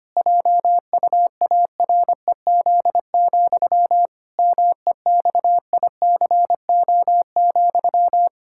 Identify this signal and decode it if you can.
Morse code, heard as a keyed tone, about 25 wpm, 700 Hz. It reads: JUAREZ, MEXICO,
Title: Pangram Challenge v2 - 25wpm